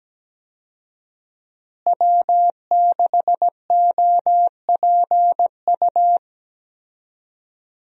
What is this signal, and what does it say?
Morse code, heard as a keyed tone, about 17 wpm, 700 Hz. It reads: W6OPU
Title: USA Call Signs - 5 Characters - Repeated 3x - 17wpm